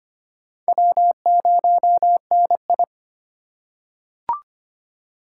Morse code, heard as a keyed tone, about 25 wpm, 700 Hz. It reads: W0NI
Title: USA Call Signs - 4 Characters - 3x Character Spacing - 25wpm